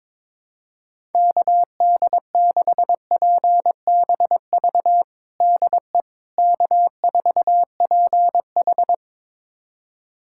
Morse code, heard as a keyed tone, about 22 wpm, 700 Hz. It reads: KD6PBV DE K4PH
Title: USA Callsign DE Callsign 22wpm